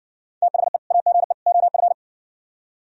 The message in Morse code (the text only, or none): THE AGE OF